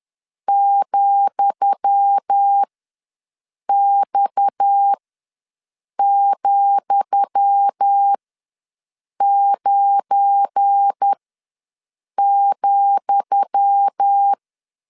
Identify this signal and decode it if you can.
Morse code, heard as a keyed tone, about 10 wpm, spaced as for 6 wpm, 800 Hz. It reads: ,X,9,
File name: LEZIONI / 10 SIGNS / 10D COMA ws 10wpm 4314